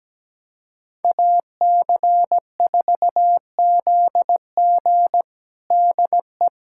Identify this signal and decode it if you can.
Morse code, heard as a keyed tone, about 17 wpm, 700 Hz. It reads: AC4ZG DE
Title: USA Callsign DE Callsign - 5 Characters - 17wpm